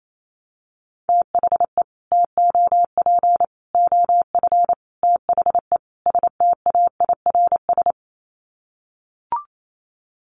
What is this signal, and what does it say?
Morse code, heard as a keyed tone, about 28 wpm, 700 Hz. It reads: THE TOP OF THE STAIRS
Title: Sets of 5 Words Encore 28wpm